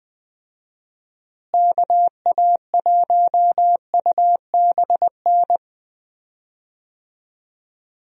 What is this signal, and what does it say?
Morse code, heard as a keyed tone, about 20 wpm, 700 Hz. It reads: KA1UBN
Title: USA Call Signs - 6 Characters - Repeated 2x - 20wpm